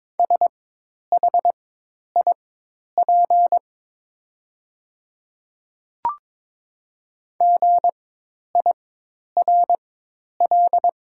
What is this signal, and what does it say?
Morse code, heard as a keyed tone, about 23 wpm, spaced as for 10 wpm, 690 Hz. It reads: SHIP GIRL
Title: 4x Character Spacing - Top 500 Words - 22wpm